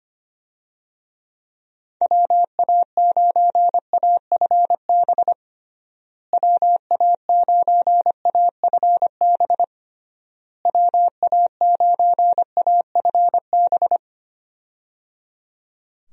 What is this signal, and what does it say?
Morse code, heard as a keyed tone, about 25 wpm, 700 Hz. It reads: WA9AFB WA9AFB WA9AFB
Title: USA Call Signs - 6 Characters - Repeated 3x - 25wpm